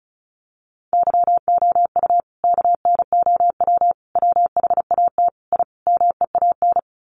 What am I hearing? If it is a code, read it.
Morse code, heard as a keyed tone, about 34 wpm, 700 Hz. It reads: YOU KNOW WHAT I MEAN
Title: Sets of 5 Words 35wpm